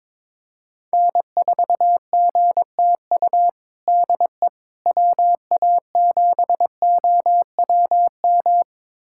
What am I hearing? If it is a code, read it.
Morse code, heard as a keyed tone, about 22 wpm, 700 Hz. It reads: N4GTU DE WA7OWM
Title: USA Callsign DE Callsign 22wpm